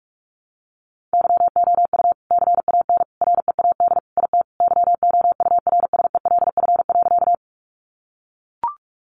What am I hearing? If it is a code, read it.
Morse code, heard as a keyed tone, about 44 wpm, 700 Hz. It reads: YOU CAN READ IT YOURSELF.
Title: Sentences from Top 600 Words 45wpm